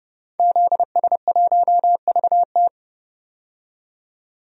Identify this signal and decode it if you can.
Morse code, heard as a keyed tone, about 30 wpm, 700 Hz. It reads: ZS1VT
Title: DXCC-100 Call Signs - 5 Characters - 30wpm